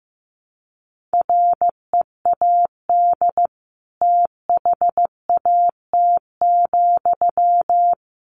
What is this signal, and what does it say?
Morse code, heard as a keyed tone, about 15 wpm, 700 Hz. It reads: READ THAT,
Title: Sentences from Top 200 Words 15wpm